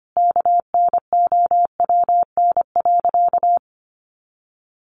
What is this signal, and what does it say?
Morse code, heard as a keyed tone, about 25 wpm, 700 Hz. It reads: KNOWN.
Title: Sentences from Top 1100 Words 25wpm